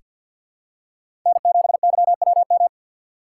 Morse code, heard as a keyed tone, about 46 wpm, 700 Hz. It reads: N7YWM
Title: USA Call Signs - 5 Characters - 6x Character Spacing - 50wpm